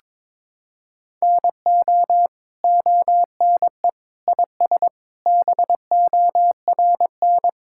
Morse code, heard as a keyed tone, about 22 wpm, 700 Hz. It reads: NO ONE IS BORN
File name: Sentences with 9-Word Phrases - 22wpm